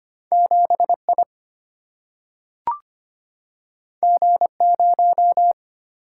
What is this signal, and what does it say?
Morse code, heard as a keyed tone, about 25 wpm, 700 Hz. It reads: 7I G0
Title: Single-Letter Single-Number 25wpm